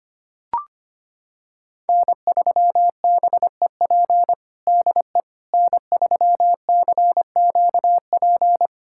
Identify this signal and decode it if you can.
Morse code, heard as a keyed tone, about 25 wpm, 700 Hz. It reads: N3BEP DE N3CQP
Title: USA Callsign DE Callsign - 5 Characters - 25wpm